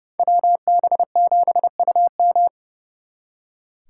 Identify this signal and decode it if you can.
Morse code, heard as a keyed tone, about 34 wpm, 700 Hz. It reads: WB7UM
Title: Speed Racing - USA 2x2 Call Signs 30wpm